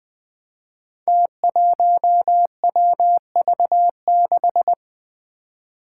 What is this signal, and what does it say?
Morse code, heard as a keyed tone, about 20 wpm, 700 Hz. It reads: T1WV6
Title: Five Challenging Letter-Number Combinations 20wpm